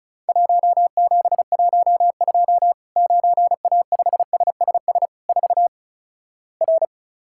Morse code, heard as a keyed tone, about 35 wpm, 700 Hz. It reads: 1712 9A5SSS 4 R
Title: Worked All Europe DX Contest - QTC Exchange - 35wpm